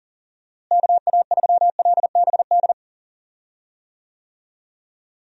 Morse code, heard as a keyed tone, about 37 wpm, 700 Hz. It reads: KA3LBD
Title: USA Call Signs - 6 Characters - Repeated 2x - 40wpm